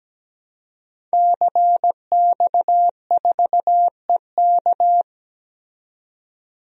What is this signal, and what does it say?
Morse code, heard as a keyed tone, about 17 wpm, 700 Hz. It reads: CX4EK
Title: DXCC-100 Call Signs - 5 Characters - 17wpm